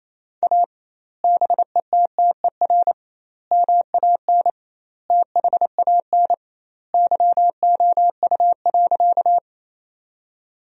Morse code, heard as a keyed tone, about 28 wpm, 700 Hz. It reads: A BETTER MAN THAN YOU.
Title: 2x Word-Spacing - Sentences from Top 400 Words - 28wpm